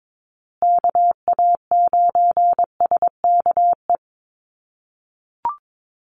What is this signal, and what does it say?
Morse code, heard as a keyed tone, about 22 wpm, 700 Hz. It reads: KA9SKE
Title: USA Call Signs Repeated 2x 22wpm